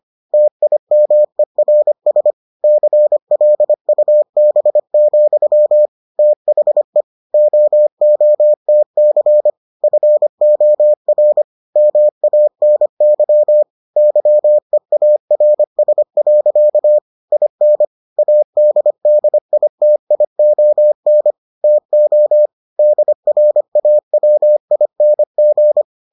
Morse code, heard as a keyed tone, about 25 wpm, 600 Hz. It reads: TIMERS CLUB, THE OOTC FOR MANY YEARS. IN ADDITION TO DRAWING